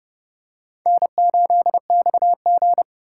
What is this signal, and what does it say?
Morse code, heard as a keyed tone, about 30 wpm, 700 Hz. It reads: N8XG
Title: USA Call Signs - 4 Characters - 2x Character Spacing - 30wpm